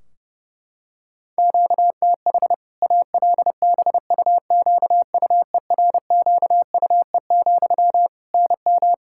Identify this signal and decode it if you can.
Morse code, heard as a keyed tone, about 30 wpm, 700 Hz. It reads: QTH ALBUQUERQUE, NM
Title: Practice QSO - Exchange 1 - Beginning - 30wpm